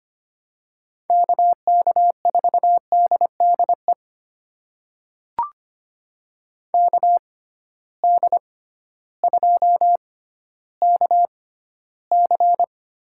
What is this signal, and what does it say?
Morse code, heard as a keyed tone, about 25 wpm, spaced as for 13 wpm, 700 Hz. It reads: KK4DDE KD2KC
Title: USA Call Signs - 6 Characters - 6x Character Spacing - 25wpm